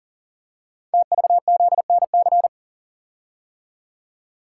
Morse code, heard as a keyed tone, about 38 wpm, 690 Hz. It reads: TVZNC
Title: Code Groups - 5 Letter:Numbers - 40wpm